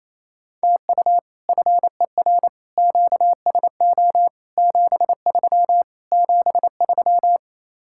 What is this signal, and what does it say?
Morse code, heard as a keyed tone, about 35 wpm, 700 Hz. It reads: TU FER QSO 73 73
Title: Speed-Racing Common QSO Phrases - 28wpm